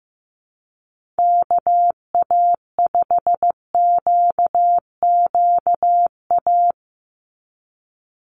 Morse code, heard as a keyed tone, about 15 wpm, 700 Hz. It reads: KA5QQA
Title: NA SOTA Chaser Callsigns 15wpm